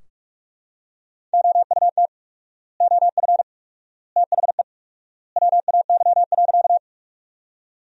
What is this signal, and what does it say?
Morse code, heard as a keyed tone, about 45 wpm, 700 Hz. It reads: OUT OF THE WAY.
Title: 4x Word-Spacing - Sentences from Top 100 Words - 45wpm